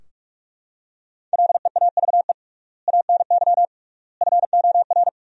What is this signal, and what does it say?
Morse code, heard as a keyed tone, about 41 wpm, 700 Hz. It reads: LEAVE ANY FOR
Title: 3x Word-Spacing - Sentences from Top 400 Words - 45wpm